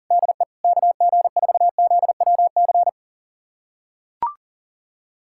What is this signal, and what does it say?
Morse code, heard as a keyed tone, about 37 wpm, 700 Hz. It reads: DE KG4ZWC
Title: USA Callsign DE Callsign - 6 Characters - 40wpm